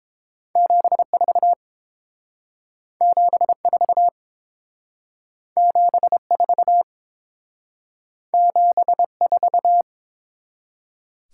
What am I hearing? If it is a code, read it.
Morse code, heard as a keyed tone, about 29 wpm, 700 Hz. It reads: 74 74 74 74